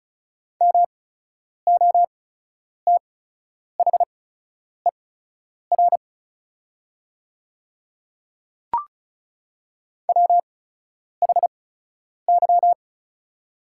Morse code, heard as a keyed tone, about 36 wpm, spaced as for 10 wpm, 700 Hz. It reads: MOTHER WHY
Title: 8x Character Spacing - Top 200 Words - 35wpm